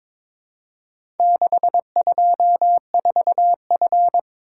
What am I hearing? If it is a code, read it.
Morse code, heard as a keyed tone, about 22 wpm, 700 Hz. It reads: 624F
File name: practice / 22wpm / Four Letter-Number Combinations - 22wpm